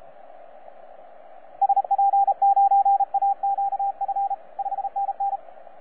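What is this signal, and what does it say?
Morse code, heard as a keyed tone, about 34 wpm, 790 Hz, with radio noise. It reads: SP9AQF 5NN
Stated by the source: T47RRC, IOTA NA-086, on 15 m CW, a DX recording by DX World.net.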